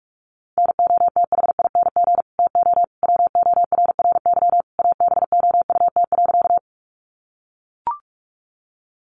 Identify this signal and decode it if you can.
Morse code, heard as a keyed tone, about 44 wpm, 700 Hz. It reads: NOTHING TO WORRY ABOUT.
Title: Sentences from Top 1100 Words 45wpm